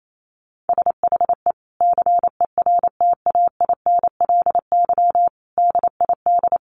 Morse code, heard as a keyed tone, about 28 wpm, 700 Hz. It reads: SHE CERTAINLY DID
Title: Sentences from Top 1100 Words 28wpm